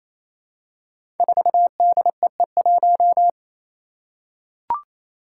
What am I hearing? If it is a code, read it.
Morse code, heard as a keyed tone, about 28 wpm, 700 Hz. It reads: 4DEE1
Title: Five Challenging Letter-Number Combinations 28wpm